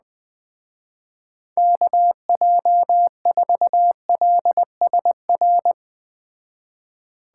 Morse code, heard as a keyed tone, about 20 wpm, 700 Hz. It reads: KJ4LSR
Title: Speed Racing - USA 2x3 Call Signs 17wpm